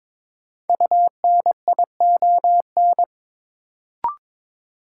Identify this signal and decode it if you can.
Morse code, heard as a keyed tone, about 22 wpm, 700 Hz. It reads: UNION
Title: Sentences from Top 1400 Words - New Word Drill - 22wpm